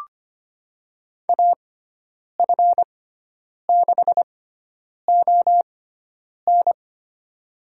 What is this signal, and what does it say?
Morse code, heard as a keyed tone, about 25 wpm, spaced as for 9 wpm, 700 Hz. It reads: AF6ON
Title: USA Call Signs - 5 Characters - 6x Character Spacing - 25wpm